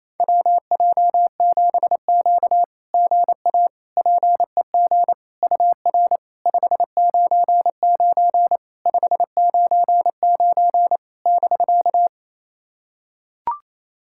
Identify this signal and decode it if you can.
Morse code, heard as a keyed tone, about 27 wpm, 690 Hz. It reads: WJ7Q GA PEG UR 599 599 <BK>
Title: SOTA QSO Practice 28wpm